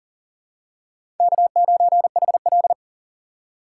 Morse code, heard as a keyed tone, about 40 wpm, 700 Hz. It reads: K9HL
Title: USA Call Signs - 4 Characters - 2x Character Spacing - 40wpm